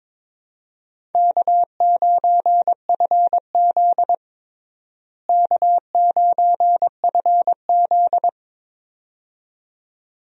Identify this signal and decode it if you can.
Morse code, heard as a keyed tone, about 22 wpm, 700 Hz. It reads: K9FZ K9FZ
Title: USA Call Signs - 4 Characters - Repeated 2x - 22wpm